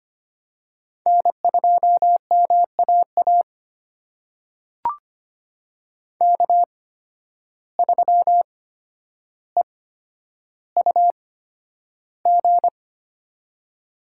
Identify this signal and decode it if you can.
Morse code, heard as a keyed tone, about 25 wpm, spaced as for 11 wpm, 700 Hz. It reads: N2MAA K3EUG
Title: USA Call Signs - 5 Characters - 8x Character Spacing - 25wpm